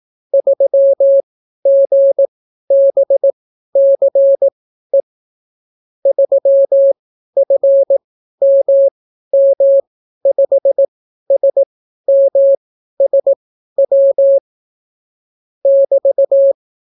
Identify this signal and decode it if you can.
Morse code, heard as a keyed tone, about 18 wpm, spaced as for 12 wpm, 550 Hz. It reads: 3GBCE 3FMM5SMSW <BT>